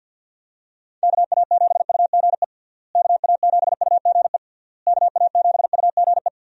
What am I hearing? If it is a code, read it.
Morse code, heard as a keyed tone, about 50 wpm, 700 Hz. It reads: KA7UGE KA7UGE KA7UGE
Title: USA Call Signs - 6 Characters - Repeated 3x - 50wpm